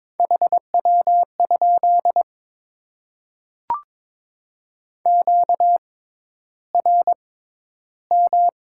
Morse code, heard as a keyed tone, about 22 wpm, spaced as for 11 wpm, 700 Hz. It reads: HW? QRM
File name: Top 90 QSO Elements - 6x Character Spacing - 22wpm